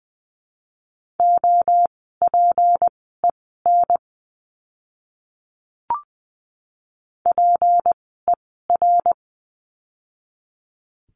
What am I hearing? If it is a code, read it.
Morse code, heard as a keyed tone, about 20 wpm, spaced as for 15 wpm, 700 Hz. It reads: OPEN PER
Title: 153 Lesson 25f - Letter p - Words 20wpm